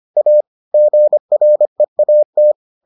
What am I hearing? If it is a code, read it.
Morse code, heard as a keyed tone, about 25 wpm, 600 Hz. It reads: A GREAT